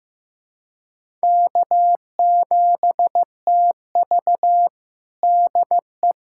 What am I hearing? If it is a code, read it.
Morse code, heard as a keyed tone, about 15 wpm, 700 Hz. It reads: K7TV DE
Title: USA Callsign DE Callsign - 4 Characters - 15wpm